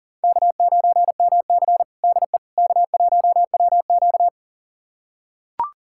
Morse code, heard as a keyed tone, about 40 wpm, 700 Hz. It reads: K9MC DE K1WQ
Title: USA Callsign DE Callsign - 4 Characters - 40wpm